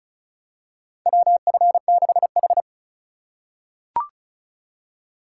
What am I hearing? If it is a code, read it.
Morse code, heard as a keyed tone, about 36 wpm, 700 Hz. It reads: WF6H
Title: USA Call Signs - 4 Characters - 2x Character Spacing - 35wpm